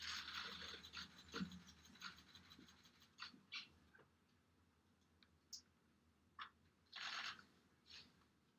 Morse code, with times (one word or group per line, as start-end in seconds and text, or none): none